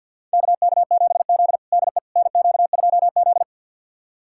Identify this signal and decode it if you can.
Morse code, heard as a keyed tone, about 52 wpm, 690 Hz. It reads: KK7Z DE NQ2Z
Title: USA Callsign DE Callsign - 4 Characters - 50wpm